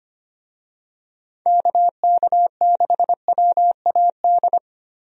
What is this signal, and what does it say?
Morse code, heard as a keyed tone, about 25 wpm, 700 Hz. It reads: KK6WAD
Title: USA Call Signs - 6 Characters - 25wpm